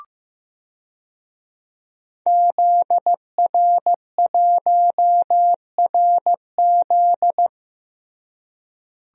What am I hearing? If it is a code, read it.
Morse code, heard as a keyed tone, about 15 wpm, 700 Hz. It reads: ZR1RZ
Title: DXCC-100 Call Signs - 5 Characters - Repeated 3x - 15wpm